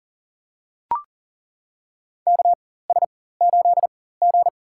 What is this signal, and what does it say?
Morse code, heard as a keyed tone, about 40 wpm, spaced as for 19 wpm, 700 Hz. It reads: KS8G
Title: USA Call Signs - 4x Character Spacing - 40wpm